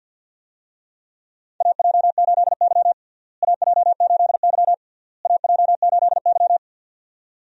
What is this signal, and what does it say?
Morse code, heard as a keyed tone, about 51 wpm, 700 Hz. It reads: AJ8Y AJ8Y AJ8Y
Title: USA Call Signs - 4 Characters - Repeated 3x - 50wpm